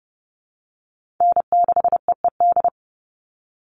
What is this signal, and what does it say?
Morse code, heard as a keyed tone, about 30 wpm, 700 Hz. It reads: N6EED
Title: Call Signs - 30wpm